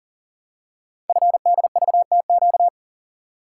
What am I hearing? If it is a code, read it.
Morse code, heard as a keyed tone, about 40 wpm, 700 Hz. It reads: FDVTQ
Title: Code Groups - 5 Letters - 40wpm